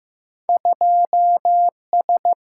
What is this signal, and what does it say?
Morse code, heard as a keyed tone, about 15 wpm, 700 Hz. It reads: 2S
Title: Single-Letter Single-Number 15wpm